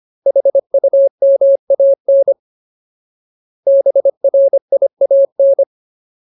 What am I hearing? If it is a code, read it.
Morse code, heard as a keyed tone, about 25 wpm, 550 Hz. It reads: HUMAN BRIAN
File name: Ultimate Word List / 25WPM Ultimate